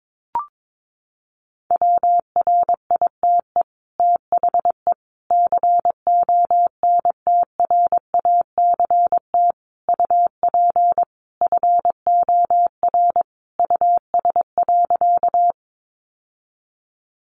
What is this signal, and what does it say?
Morse code, heard as a keyed tone, about 22 wpm, 700 Hz. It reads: WRITE THE CONTRACT UP FOR US.
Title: Sentences from Top 1400 Words 22wpm